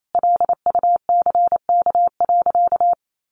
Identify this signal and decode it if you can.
Morse code, heard as a keyed tone, about 28 wpm, 700 Hz. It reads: LUCK.
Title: Sentences from Top 1500 Words 28wpm